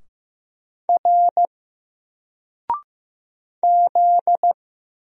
Morse code, heard as a keyed tone, about 15 wpm, 700 Hz. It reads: R Z
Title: Single Character (Letter:Number:Punctuation:Prosign) 15wpm